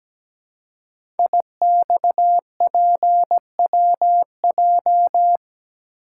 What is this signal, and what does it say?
Morse code, heard as a keyed tone, about 17 wpm, 700 Hz. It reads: IXPWJ
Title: Code Groups - 5 Letters - 17wpm